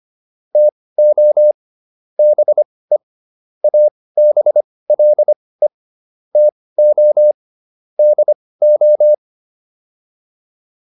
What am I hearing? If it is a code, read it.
Morse code, heard as a keyed tone, about 25 wpm, spaced as for 18 wpm, 600 Hz. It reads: TO BE ABLE TO DO